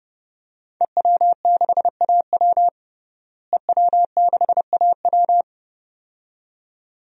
Morse code, heard as a keyed tone, about 30 wpm, 700 Hz. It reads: EW6AW EW6AW
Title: DXCC-100 Call Signs - 5 Characters - Repeated 2x - 30wpm